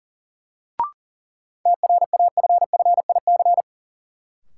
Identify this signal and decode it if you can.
Morse code, heard as a keyed tone, about 40 wpm, 700 Hz. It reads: TRAFFIC
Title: Sentences from Top 1300 Words - New Word Drill Alt - 40wpm